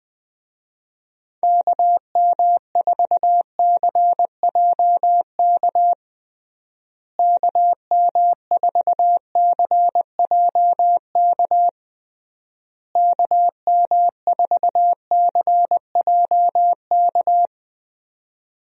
Morse code, heard as a keyed tone, about 20 wpm, 700 Hz. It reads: KM4CJK KM4CJK KM4CJK
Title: USA Call Signs - 6 Characters - Repeated 3x - 20wpm